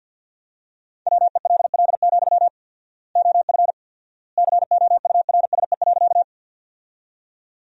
Morse code, heard as a keyed tone, about 52 wpm, 700 Hz. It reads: WELL, OF COURSE.
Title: 4x Word-Spacing - Sentences from Top 500 Words - 50wpm